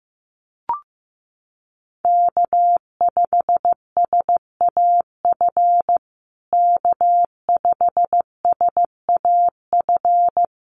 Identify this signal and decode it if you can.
Morse code, heard as a keyed tone, about 15 wpm, 700 Hz. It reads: K5SAF K5SAF
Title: USA Call Signs Repeated 2x 15wpm